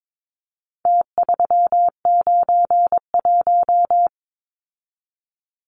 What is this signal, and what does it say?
Morse code, heard as a keyed tone, about 22 wpm, 700 Hz. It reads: T391
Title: Numbers - Contest Style - 4-Digit 22wpm